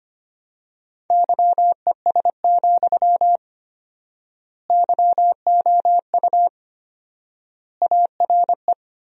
Morse code, heard as a keyed tone, about 25 wpm, 700 Hz. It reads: YES, YOU ARE
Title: 4x Word-Spacing - Sentences from Top 500 Words - 25wpm